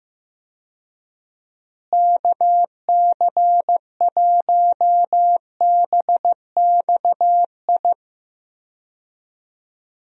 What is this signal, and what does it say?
Morse code, heard as a keyed tone, about 15 wpm, 700 Hz. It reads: KC1BXI
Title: USA Call Signs - 6 Characters - Repeated 2x - 15wpm